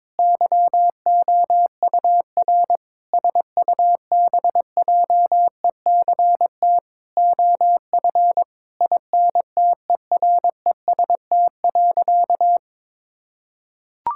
Morse code, heard as a keyed tone, about 22 wpm, 700 Hz. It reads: YOUR SUBJECT OF INTEREST.
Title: Sentences from Top 1000 Words - Review Part 5 - 22wpm